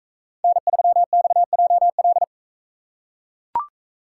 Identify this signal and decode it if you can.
Morse code, heard as a keyed tone, about 40 wpm, 700 Hz. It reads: N3XJL